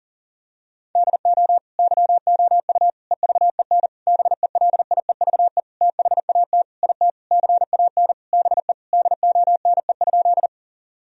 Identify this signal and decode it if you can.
Morse code, heard as a keyed tone, about 40 wpm, 700 Hz. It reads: DO YOU EVEN BELIEVE THAT IT CAN BE DONE?
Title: Sentences from Top 1000 Words - Review Part 1 - 40wpm